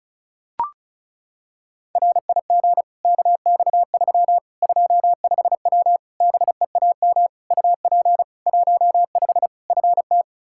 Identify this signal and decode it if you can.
Morse code, heard as a keyed tone, about 36 wpm, 700 Hz. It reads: RIG KX3 25W BEAM UP 15 FT
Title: K1USN SST QSO Practice 35wpm